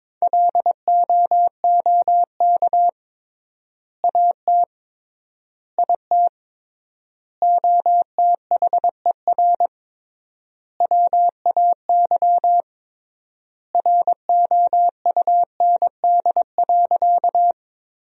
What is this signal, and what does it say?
Morse code, heard as a keyed tone, about 21 wpm, 700 Hz. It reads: LOOK AT IT OTHER WAY ROUND.
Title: 3x Word-Spacing - Sentences from Top 200 Words - 22wpm